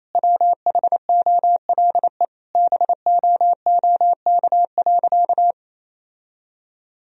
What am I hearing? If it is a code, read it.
Morse code, heard as a keyed tone, about 28 wpm, 700 Hz. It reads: WHOLE BOOK.
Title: Sentences from Top 1000 Words - Review Part 1 - 28wpm